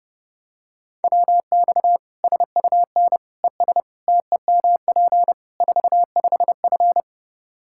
Morse code, heard as a keyed tone, about 32 wpm, 700 Hz. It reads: WX SUN ES TEMP 45F
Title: Practice QSO - Exchange 2 - Info - 30wpm